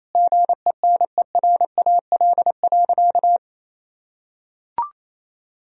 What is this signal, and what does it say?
Morse code, heard as a keyed tone, about 28 wpm, 700 Hz. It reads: GENERAL.
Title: Sentences from Top 1000 Words - Review Part 3 - 28wpm